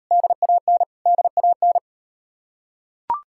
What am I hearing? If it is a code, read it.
Morse code, heard as a keyed tone, about 36 wpm, 700 Hz. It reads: DAN DAN